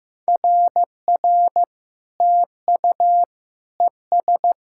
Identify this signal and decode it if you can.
Morse code, heard as a keyed tone, about 15 wpm, 700 Hz. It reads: RR TU ES
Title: SOTA QSO Practice 15wpm